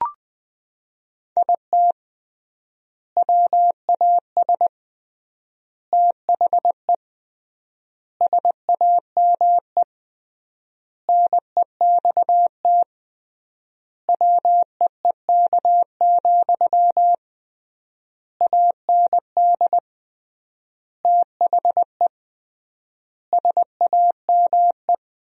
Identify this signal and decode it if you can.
Morse code, heard as a keyed tone, about 20 wpm, 700 Hz. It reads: IT WAS THE SAME NEXT WEEK, AND THE SAME
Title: 3x Word-Spacing - Sentences from Top 500 Words - 20wpm